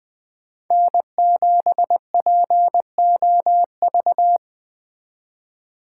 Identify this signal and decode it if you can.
Morse code, heard as a keyed tone, about 21 wpm, 700 Hz. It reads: N7POV
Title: Speed Racing - USA Call Signs 20wpm